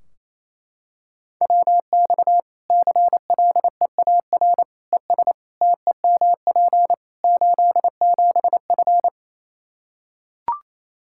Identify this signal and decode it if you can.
Morse code, heard as a keyed tone, about 27 wpm, 700 Hz. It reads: WX CLEAR ES TEMP 87F
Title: Practice QSO - Exchange 2 - Info - 28wpm